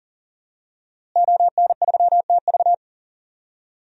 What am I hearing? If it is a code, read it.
Morse code, heard as a keyed tone, about 37 wpm, 700 Hz. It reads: ON3TV
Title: DXCC-100 Call Signs - 5 Characters - Repeated 3x - 40wpm